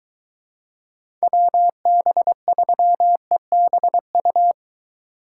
Morse code, heard as a keyed tone, about 23 wpm, 700 Hz. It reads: WB3EBU